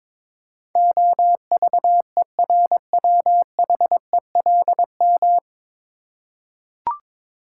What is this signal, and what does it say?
Morse code, heard as a keyed tone, about 22 wpm, 700 Hz. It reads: OVERWHELM
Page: Morse Code Ninja practice 22wpm